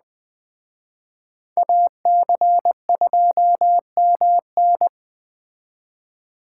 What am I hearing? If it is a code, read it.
Morse code, heard as a keyed tone, about 20 wpm, 700 Hz. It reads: AC2MN
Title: Speed Racing - USA 2x2 Call Signs 17wpm